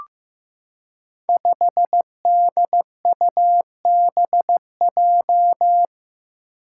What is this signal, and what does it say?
Morse code, heard as a keyed tone, about 15 wpm, 700 Hz. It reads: 5DUBJ
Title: Five Challenging Letter-Number Combinations 15wpm